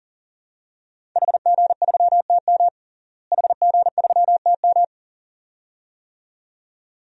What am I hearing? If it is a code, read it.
Morse code, heard as a keyed tone, about 37 wpm, 700 Hz. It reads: HG3TM HG3TM
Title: DXCC-100 Call Signs - 5 Characters - Repeated 2x - 40wpm